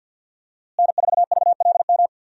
Morse code, heard as a keyed tone, about 51 wpm, 700 Hz. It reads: N4VLM
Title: USA Call Signs - 5 Characters - Repeated 2x - 50wpm